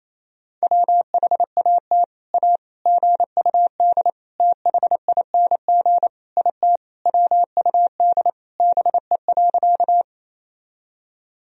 Morse code, heard as a keyed tone, about 28 wpm, 700 Hz. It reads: WHAT A GUD THING IT WUD BE.
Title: Sentences from Top 200 Words and Abbreviations 28wpm